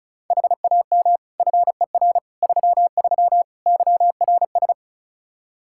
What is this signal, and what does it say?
Morse code, heard as a keyed tone, about 36 wpm, 700 Hz. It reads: HAM FER 33 YRS
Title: Practice QSO - Exchange 3 - More Info - 35wpm